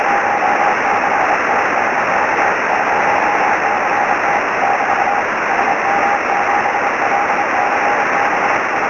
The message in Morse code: none

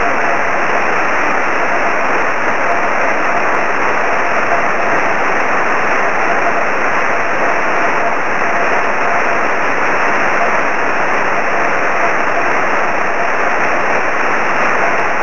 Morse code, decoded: none

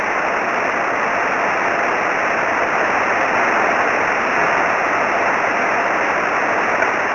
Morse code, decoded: none

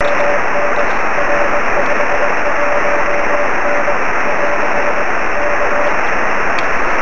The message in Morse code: none

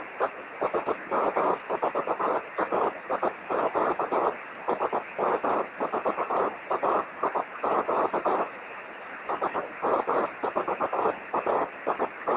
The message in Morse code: none

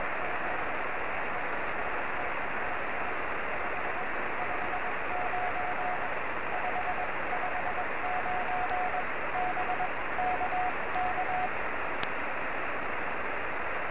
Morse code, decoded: none

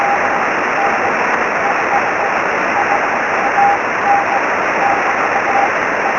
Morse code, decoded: none

